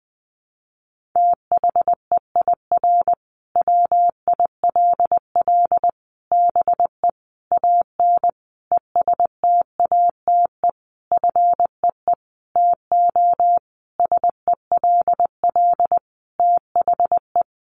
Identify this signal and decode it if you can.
Morse code, heard as a keyed tone, about 20 wpm, 700 Hz. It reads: THEIR WILL BE AN ESTATE FEE TO SELL THE
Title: Sentences from Top 1500 Words 20wpm